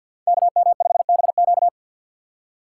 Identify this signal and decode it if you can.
Morse code, heard as a keyed tone, about 52 wpm, 700 Hz. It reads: KM5BQ
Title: USA Call Signs - 5 Characters - 8x Character Spacing - 50wpm